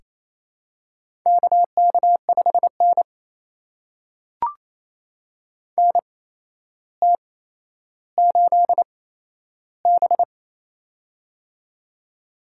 Morse code, heard as a keyed tone, about 28 wpm, spaced as for 12 wpm, 700 Hz. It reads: KK5N NT8B
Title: USA Call Signs - 4 Characters - 8x Character Spacing - 28wpm